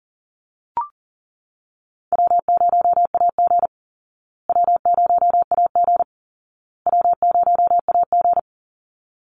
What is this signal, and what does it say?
Morse code, heard as a keyed tone, about 40 wpm, 700 Hz. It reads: W0AG W0AG W0AG
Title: Easy Call Signs Repeated 3x - 40wpm